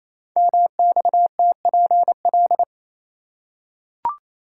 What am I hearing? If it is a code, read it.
Morse code, heard as a keyed tone, about 27 wpm, 700 Hz. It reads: MXTPL
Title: Code Groups - 5 Letters - 28wpm